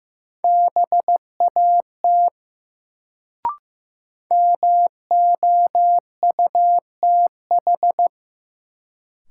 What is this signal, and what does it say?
Morse code, heard as a keyed tone, about 15 wpm, 700 Hz. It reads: BAT MOUTH
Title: Sentences from Top 1500 Words - New Word Drill Alt - 15wpm